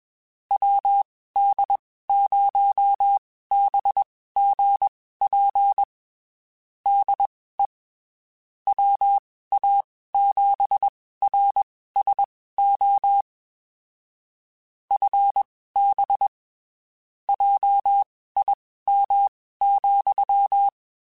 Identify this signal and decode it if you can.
Morse code, heard as a keyed tone, about 21 wpm, spaced as for 15 wpm, 800 Hz. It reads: WD0BGP DE WA7RSO FB JIM,
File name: (spaced) / QSOs Spaced-mp3s / WD0BGP 1 @20 sp